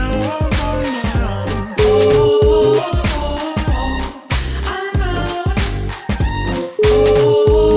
Ringback tone (Europe, 425 Hz), with noise.